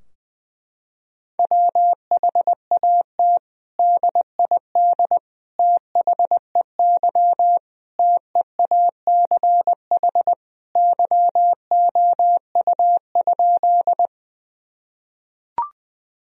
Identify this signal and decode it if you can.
Morse code, heard as a keyed tone, about 20 wpm, 700 Hz. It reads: WHAT DID THEY TEACH YOU?
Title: Sentences from Top 1000 Words - Review Part 2 - 20wpm